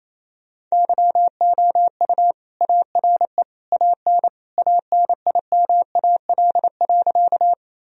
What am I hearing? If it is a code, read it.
Morse code, heard as a keyed tone, about 28 wpm, 700 Hz. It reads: YOU ARE AN ANIMAL.
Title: Sentences from Top 1000 Words - Review Part 1 - 28wpm